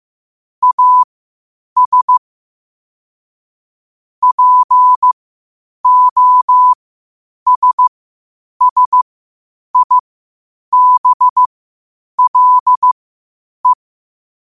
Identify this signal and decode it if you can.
Morse code, heard as a keyed tone, about 15 wpm, spaced as for 8 wpm, 1000 Hz. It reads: AS POSSIBLE
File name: qotdfiles / 8wpm